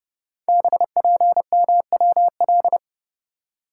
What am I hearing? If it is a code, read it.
Morse code, heard as a keyed tone, about 31 wpm, 700 Hz. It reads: BPMWL
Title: Code Groups - 5 Letters - 30wpm